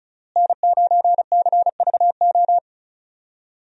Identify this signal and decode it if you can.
Morse code, heard as a keyed tone, about 34 wpm, 700 Hz. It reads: N9CVO
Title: USA Call Signs - 5 Characters - Repeated 3x - 35wpm